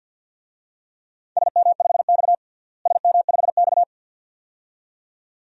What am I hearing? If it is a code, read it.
Morse code, heard as a keyed tone, about 47 wpm, 700 Hz. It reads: SM5X SM5X
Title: DXCC-100 Call Signs - 4 Characters - Repeated 2x - 50wpm